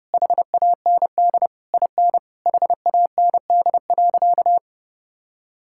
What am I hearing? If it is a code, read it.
Morse code, heard as a keyed tone, about 32 wpm, 700 Hz. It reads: HAND IN HAND.
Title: Sentences from Top 1000 Words - Review Part 5 - 30wpm